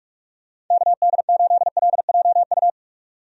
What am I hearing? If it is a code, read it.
Morse code, heard as a keyed tone, about 45 wpm, 700 Hz. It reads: KD8LJU
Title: USA Call Signs - 6 Characters - 45wpm